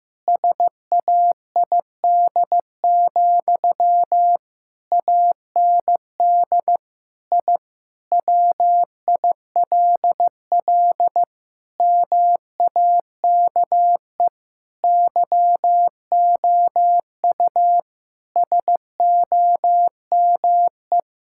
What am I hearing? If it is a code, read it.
Morse code, heard as a keyed tone, about 15 wpm, 700 Hz. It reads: SAID, AND I WILL MAKE YOU SOME
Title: Sentences from Top 1000 Words - Review Part 6 - 15wpm